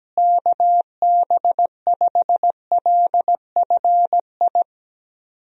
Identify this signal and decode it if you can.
Morse code, heard as a keyed tone, about 17 wpm, 700 Hz. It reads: KB5LFI
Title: USA Call Signs - 6 Characters - Repeated 3x - 17wpm